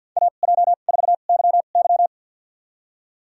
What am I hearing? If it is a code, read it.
Morse code, heard as a keyed tone, about 48 wpm, spaced as for 35 wpm, 700 Hz. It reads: AJ4YY
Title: USA Call Signs - 5 Characters - 2x Character Spacing - 50wpm